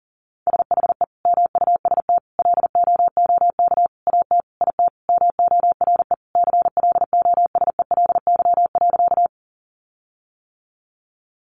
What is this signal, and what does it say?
Morse code, heard as a keyed tone, about 40 wpm, 700 Hz. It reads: SHE MUST LOOK AT IT MORE CLOSELY.
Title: Sentences from Top 1400 Words 40wpm